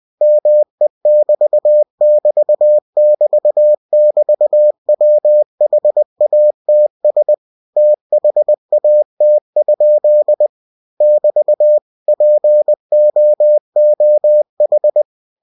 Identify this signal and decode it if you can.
Morse code, heard as a keyed tone, about 20 wpm, 600 Hz. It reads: ME<BT><BT><BT><BT>WHATS THAT? <BT> POOH
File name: cw-books / winnie-the-pooh / winnie-20wpm-page-0024